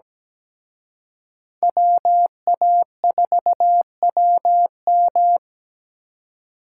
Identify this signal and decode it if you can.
Morse code, heard as a keyed tone, about 20 wpm, 700 Hz. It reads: WA4WM
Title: Speed Racing - USA 2x2 Call Signs 17wpm